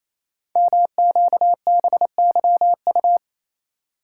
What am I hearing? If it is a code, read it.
Morse code, heard as a keyed tone, about 27 wpm, 700 Hz. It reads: MQBYU
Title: Code Groups - 5 Letters - 28wpm